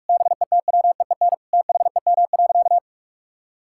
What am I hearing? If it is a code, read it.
Morse code, heard as a keyed tone, about 41 wpm, 700 Hz. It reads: BETWEEN THEM.
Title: Sentences from Top 1000 Words - Review Part 6 - 45wpm